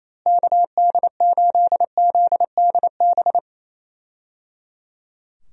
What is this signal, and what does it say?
Morse code, heard as a keyed tone, about 28 wpm, 700 Hz. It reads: KD8ZDB
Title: USA Call Signs - 6 Characters - Repeated 3x - 28wpm